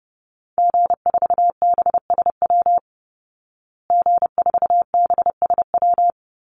Morse code, heard as a keyed tone, about 28 wpm, 700 Hz. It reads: G4BSW G4BSW
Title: Call Signs Repeated 3x - 30wpm